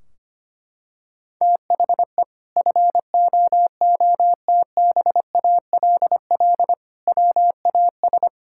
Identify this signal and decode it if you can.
Morse code, heard as a keyed tone, about 25 wpm, 700 Hz. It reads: THE FOOTBALL WAS
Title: Sentences from Top 1600 Words 25wpm